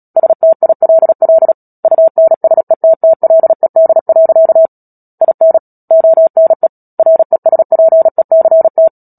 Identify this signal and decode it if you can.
Morse code, heard as a keyed tone, about 37 wpm, 650 Hz. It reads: STILL UNSETTLED. IN ONE RESPECT
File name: Book Two / 10-Chapter 10 - Epilogue - 36 WPM